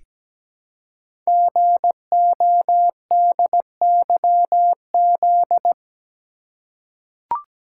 Morse code, heard as a keyed tone, about 17 wpm, 700 Hz. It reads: GODYZ
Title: Code Groups - 5 Letters - 17wpm